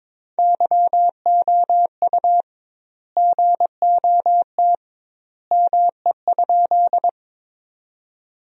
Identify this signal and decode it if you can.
Morse code, heard as a keyed tone, about 21 wpm, 700 Hz. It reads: YOU GOT ME?
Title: 2x Word-Spacing - Sentences from Top 300 Words - 22wpm